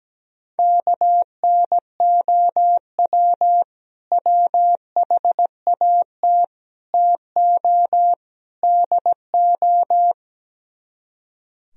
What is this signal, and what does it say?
Morse code, heard as a keyed tone, about 17 wpm, 700 Hz. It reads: KNOW WHAT TO DO
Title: Speed-Racing - Sets of 4 Words 17wpm